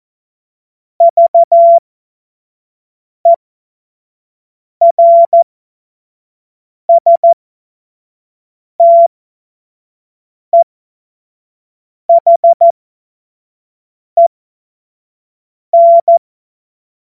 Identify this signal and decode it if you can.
Morse code, heard as a keyed tone, about 14 wpm, spaced as for 5 wpm, 680 Hz. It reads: VERSTEHEN
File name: Morsekurs / Witze / Witz 15. Februar 2022 25 F70